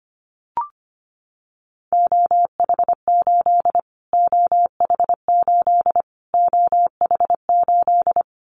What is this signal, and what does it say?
Morse code, heard as a keyed tone, about 25 wpm, 700 Hz. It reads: OH8 OH8 OH8
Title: Callsign Prefixes for Top 25 DXCC - Repeated 3x - 25wpm